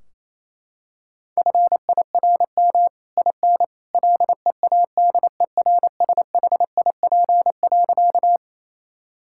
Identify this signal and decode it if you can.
Morse code, heard as a keyed tone, about 28 wpm, 700 Hz. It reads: FIRM IN LEADERSHIP.
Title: Sentences from Top 1600 Words 28wpm